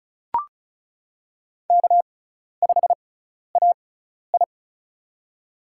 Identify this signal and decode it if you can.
Morse code, heard as a keyed tone, about 34 wpm, spaced as for 12 wpm, 700 Hz. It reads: K5AI
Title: USA Call Signs - 4 Characters - 6x Character Spacing - 35wpm